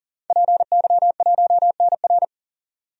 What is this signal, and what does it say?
Morse code, heard as a keyed tone, about 39 wpm, 700 Hz. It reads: PY1NR
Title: DXCC-100 Call Signs - 5 Characters - 40wpm